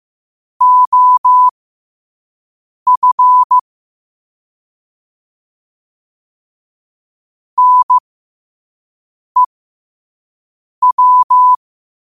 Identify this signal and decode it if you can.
Morse code, heard as a keyed tone, about 15 wpm, spaced as for 5 wpm, 1000 Hz. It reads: OF NEW
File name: qotdfiles / 5wpm